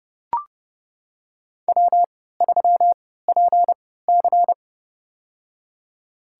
Morse code, heard as a keyed tone, about 30 wpm, spaced as for 17 wpm, 700 Hz. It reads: W3PC
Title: USA Call Signs - 4 Characters - 3x Character Spacing - 30wpm